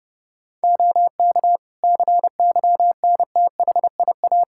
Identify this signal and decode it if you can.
Morse code, heard as a keyed tone, about 30 wpm, 700 Hz. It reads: OK CYNTHIA
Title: K1USN SST QSO Practice 30wpm